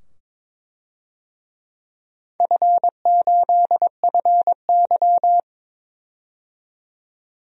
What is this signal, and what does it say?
Morse code, heard as a keyed tone, about 22 wpm, 690 Hz. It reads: F8FY
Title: DXCC-100 Call Signs - 4 Characters - 22wpm